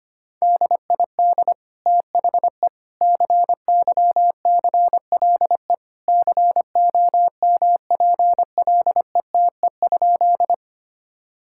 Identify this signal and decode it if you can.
Morse code, heard as a keyed tone, about 25 wpm, 700 Hz. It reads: DID THE CYCLE COMPLETE?
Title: Sentences from Top 1000 Words - Review Part 6 - 25wpm